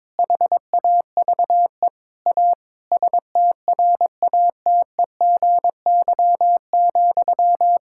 Morse code, heard as a keyed tone, about 22 wpm, 700 Hz. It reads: HAVE A STRATEGY,
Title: Sentences from Top 1000 Words - Review Part 5 - 22wpm